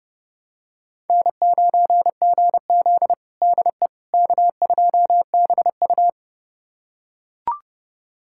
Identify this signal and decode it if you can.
Morse code, heard as a keyed tone, about 30 wpm, 700 Hz. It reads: N9GZ DE K2BU
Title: USA Callsign DE Callsign - 4 Characters - 30wpm